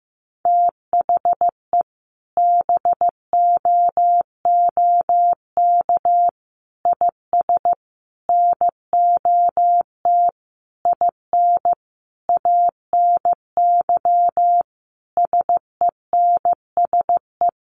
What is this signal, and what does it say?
Morse code, heard as a keyed tone, about 15 wpm, 700 Hz. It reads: THE BOOK IS NOT IN ANY SENSE
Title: Sentences from Top 1300 Words 15wpm